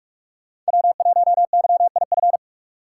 Morse code, heard as a keyed tone, about 48 wpm, 700 Hz. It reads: W1YIF
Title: USA Call Signs - 2x Character Spacing - 45wpm